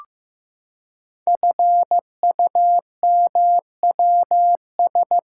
Morse code, heard as a keyed tone, about 15 wpm, 700 Hz. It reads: FUMWS
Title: Code Groups - 5 Letters - 15wpm